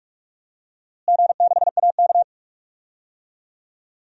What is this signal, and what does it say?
Morse code, heard as a keyed tone, about 41 wpm, 700 Hz. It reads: G6AK
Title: DXCC-100 Call Signs - 4 Characters - Repeated 2x - 45wpm